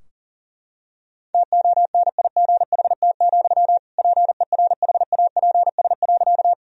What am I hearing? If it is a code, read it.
Morse code, heard as a keyed tone, about 40 wpm, 700 Hz. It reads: TONIGHT, PERHAPS.
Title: Sentences from Top 1000 Words - Review Part 3 - 40wpm